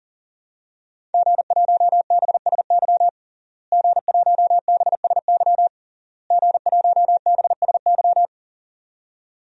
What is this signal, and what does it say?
Morse code, heard as a keyed tone, about 40 wpm, 700 Hz. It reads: G1BSY G1BSY G1BSY
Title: DXCC-100 Call Signs - 5 Characters - Repeated 3x - 40wpm